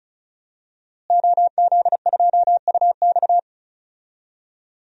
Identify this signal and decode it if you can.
Morse code, heard as a keyed tone, about 34 wpm, 700 Hz. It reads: OZ2UX
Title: DXCC-100 Call Signs - 5 Characters - Repeated 3x - 35wpm